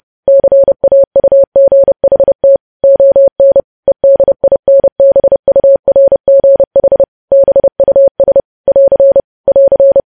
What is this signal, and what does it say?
Morse code, heard as a keyed tone, about 30 wpm, 550 Hz. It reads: CAUGHT ON EDINBURGH BUS <AR> <AR>
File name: podcast / News headlines-1x-30x30